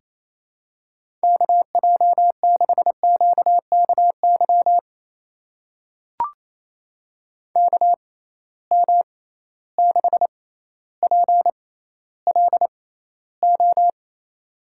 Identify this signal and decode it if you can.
Morse code, heard as a keyed tone, about 28 wpm, spaced as for 14 wpm, 700 Hz. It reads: KJ6QKY KM6PLO
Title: USA Call Signs - 6 Characters - 6x Character Spacing - 28wpm